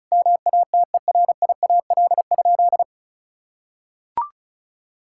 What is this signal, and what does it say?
Morse code, heard as a keyed tone, about 36 wpm, 700 Hz. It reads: MATERIAL?
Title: Sentences from Top 1000 Words - Review Part 5 - 35wpm